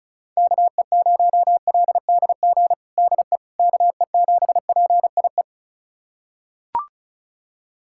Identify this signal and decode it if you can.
Morse code, heard as a keyed tone, about 36 wpm, 700 Hz. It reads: KE0LDG DE KE7PIE
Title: USA Callsign DE Callsign - 6 Characters - 35wpm